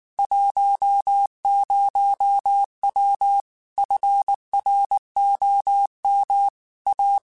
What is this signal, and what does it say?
Morse code, heard as a keyed tone, about 19 wpm, 800 Hz. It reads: 10W FROM A
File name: QSOs / QSOs-mp3s / kc7sac@18